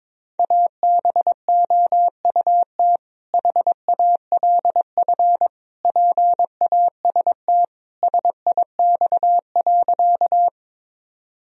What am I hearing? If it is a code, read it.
Morse code, heard as a keyed tone, about 22 wpm, 690 Hz. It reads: ABOUT HALF PAST SIX.
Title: Sentences from Top 1000 Words - Review Part 3 - 22wpm